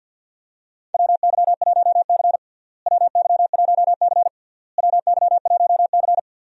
Morse code, heard as a keyed tone, about 50 wpm, 700 Hz. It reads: WY1C WY1C WY1C
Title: USA Call Signs - 4 Characters - Repeated 3x - 50wpm